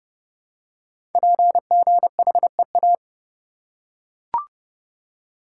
Morse code, heard as a keyed tone, about 31 wpm, 700 Hz. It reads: PGHEA
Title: Code Groups - 5 Letters - 30wpm